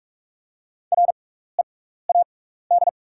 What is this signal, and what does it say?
Morse code, heard as a keyed tone, about 44 wpm, spaced as for 15 wpm, 700 Hz. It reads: READ
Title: 6x Character Spacing - Top 200 Words - 45wpm